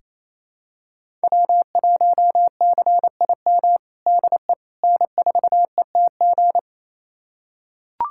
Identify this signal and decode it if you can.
Morse code, heard as a keyed tone, about 27 wpm, 700 Hz. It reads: W1CIM DE N4ETG
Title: USA Callsign DE Callsign - 5 Characters - 28wpm